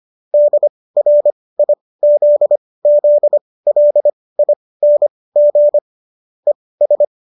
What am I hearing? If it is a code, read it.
Morse code, heard as a keyed tone, about 25 wpm, spaced as for 18 wpm, 590 Hz. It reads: DRIZZLING ES
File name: qso218 18